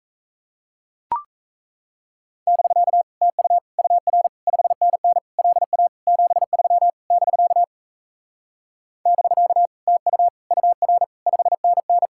Morse code, heard as a keyed tone, about 37 wpm, 700 Hz. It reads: <BK> TU UR 5NN LA 73 <BK> <BK> TU UR 5NN